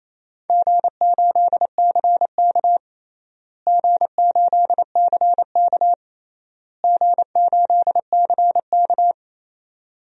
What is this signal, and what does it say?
Morse code, heard as a keyed tone, about 28 wpm, 700 Hz. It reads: G8CK G8CK G8CK
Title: DXCC-100 Call Signs - 4 Characters - Repeated 3x - 28wpm